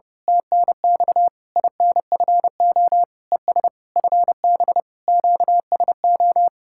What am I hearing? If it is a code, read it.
Morse code, heard as a keyed tone, about 28 wpm, 700 Hz. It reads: TNX INFO ES FB QSO
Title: Speed-Racing Common QSO Phrases Encore - 30wpm